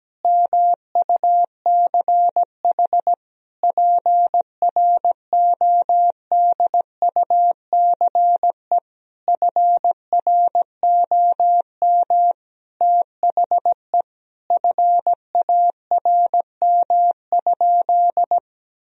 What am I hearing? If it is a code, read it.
Morse code, heard as a keyed tone, about 17 wpm, 700 Hz. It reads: MUCH PRODUCE FROM THE FARM?
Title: Sentences from Top 1000 Words - Review Part 3 - 17wpm